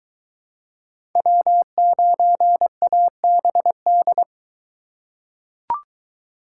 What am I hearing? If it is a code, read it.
Morse code, heard as a keyed tone, about 21 wpm, 700 Hz. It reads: W9ABD